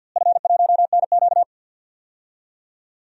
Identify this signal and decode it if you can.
Morse code, heard as a keyed tone, about 53 wpm, 700 Hz. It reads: F1NQ
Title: DXCC-100 Call Signs - 4 Characters - Repeated 2x - 50wpm